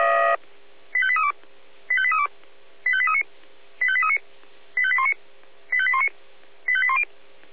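Five-tone selective call (ZVEI), heard as quick runs of five tones, tones of about 70 ms. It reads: 85832 85832 85839 85839 85819 85819 85810
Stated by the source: ZVEI five-tone selective call received on an SDR.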